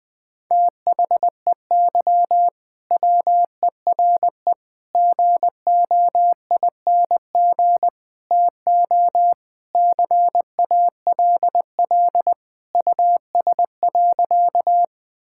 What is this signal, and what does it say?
Morse code, heard as a keyed tone, about 20 wpm, 700 Hz. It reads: THEY WERE GOING TO CALL US.
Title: Sentences with 4-Word Phrases - 20wpm